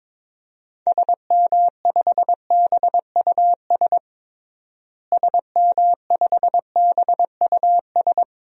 Morse code, heard as a keyed tone, about 22 wpm, 700 Hz. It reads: SM5BUS SM5BUS
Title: DXCC-100 Call Signs - 6 Characters - Repeated 2x - 22wpm